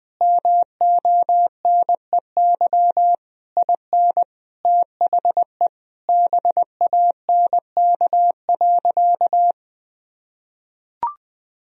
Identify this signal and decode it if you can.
Morse code, heard as a keyed tone, about 21 wpm, 700 Hz. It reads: MONEY IN THE BANK.
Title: Sentences from Top 1000 Words - Review Part 4 - 20wpm